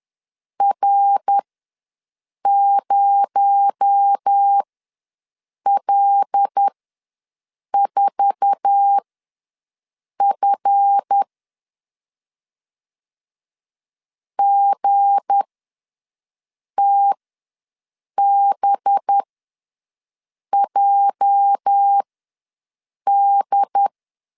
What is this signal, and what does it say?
Morse code, heard as a keyed tone, about 10 wpm, spaced as for 6 wpm, 800 Hz. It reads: R0L4F GTBJD
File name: LEZIONI / 9 2 ws 10wpm 4130